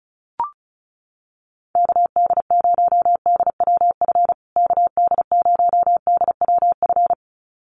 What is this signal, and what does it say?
Morse code, heard as a keyed tone, about 34 wpm, 700 Hz. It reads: KD0DWF KD0DWF
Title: USA Call Signs Repeated 2x 35wpm